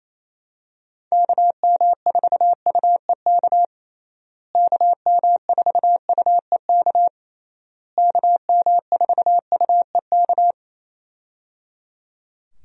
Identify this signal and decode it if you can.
Morse code, heard as a keyed tone, about 28 wpm, 700 Hz. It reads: KM4UEK KM4UEK KM4UEK
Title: USA Call Signs - 6 Characters - Repeated 3x - 28wpm